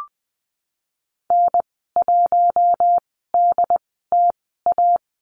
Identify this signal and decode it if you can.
Morse code, heard as a keyed tone, about 20 wpm, spaced as for 15 wpm, 700 Hz. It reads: N1DTA
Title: 088 Lesson 15g - Number 5 - Call Signs 20wpm